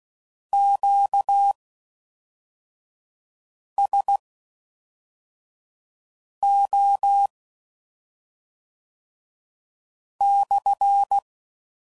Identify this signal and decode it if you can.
Morse code, heard as a keyed tone, about 16 wpm, spaced as for 3 wpm, 800 Hz. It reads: QSO/